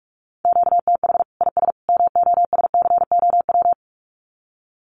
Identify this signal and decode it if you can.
Morse code, heard as a keyed tone, about 44 wpm, 700 Hz. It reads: QTH IS MOSCOW
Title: Common QSO Phrases 45wpm